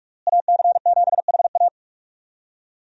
Morse code, heard as a keyed tone, about 45 wpm, 700 Hz. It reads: AC7HA
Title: USA Call Signs - 5 Characters - Repeated 2x - 45wpm